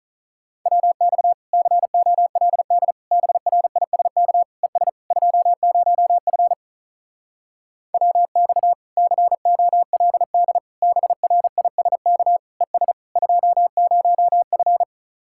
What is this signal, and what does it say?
Morse code, heard as a keyed tone, about 45 wpm, 700 Hz. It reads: WX COLD BRISK ES 20F WX COLD BRISK ES 20F